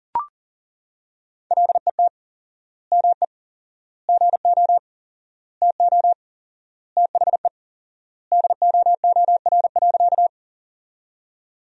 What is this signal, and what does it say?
Morse code, heard as a keyed tone, about 40 wpm, 700 Hz. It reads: LET ME GO TO THE DOOR.
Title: 4x Word-Spacing - Sentences from Top 300 Words - 40wpm